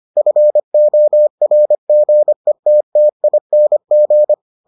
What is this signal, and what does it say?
Morse code, heard as a keyed tone, about 25 wpm, 600 Hz. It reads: FORGETTING